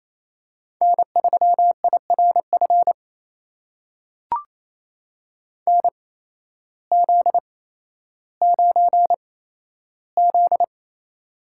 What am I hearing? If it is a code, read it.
Morse code, heard as a keyed tone, about 28 wpm, spaced as for 13 wpm, 700 Hz. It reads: N3IRF NZ9Z
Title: USA Call Signs - 8x Character Spacing - 28wpm